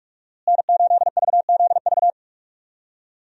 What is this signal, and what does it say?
Morse code, heard as a keyed tone, about 44 wpm, 700 Hz. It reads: N8VZV
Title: USA Call Signs - 5 Characters - Repeated 3x - 45wpm